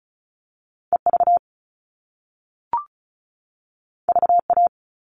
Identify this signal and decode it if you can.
Morse code, heard as a keyed tone, about 36 wpm, 690 Hz. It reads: EV VA
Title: All Cut Numbers - 2-Digit 35wpm